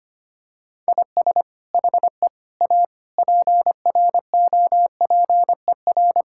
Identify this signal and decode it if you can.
Morse code, heard as a keyed tone, about 25 wpm, 700 Hz. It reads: IS HE A PROPER
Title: Sentences from Top 1000 Words - Review Part 5 - 25wpm